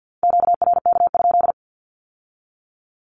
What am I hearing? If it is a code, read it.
Morse code, heard as a keyed tone, about 50 wpm, 700 Hz. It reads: QRK?
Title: Q-Codes with Spoken Meaning - 50wpm